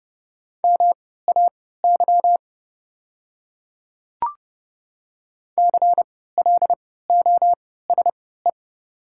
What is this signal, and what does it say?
Morse code, heard as a keyed tone, about 30 wpm, spaced as for 17 wpm, 690 Hz. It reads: MAY CLOSE
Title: 3x Character Spacing - Top 300 Words - 30wpm